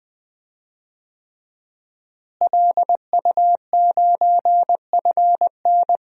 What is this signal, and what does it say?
Morse code, heard as a keyed tone, about 20 wpm, 700 Hz. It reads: LU9FN
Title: DXCC-100 Call Signs - 5 Characters - Repeated 3x - 20wpm